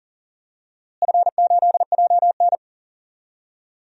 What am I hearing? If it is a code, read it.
Morse code, heard as a keyed tone, about 40 wpm, 700 Hz. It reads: F8JN
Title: DXCC-100 Call Signs - 4 Characters - Repeated 3x - 40wpm